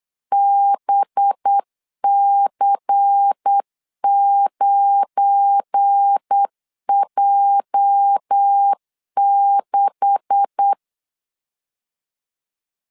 Morse code, heard as a keyed tone, about 8 wpm, 800 Hz. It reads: BC9J6
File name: CW / DET / 1 9 1500C S B 8wpm 5624